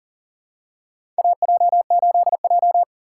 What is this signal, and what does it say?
Morse code, heard as a keyed tone, about 40 wpm, 700 Hz. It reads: AJ8J
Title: USA Call Signs - 4 Characters - 4x Character Spacing - 40wpm